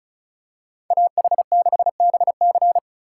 Morse code, heard as a keyed tone, about 36 wpm, 700 Hz. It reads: AH6BC
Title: USA Call Signs - 2x Character Spacing - 35wpm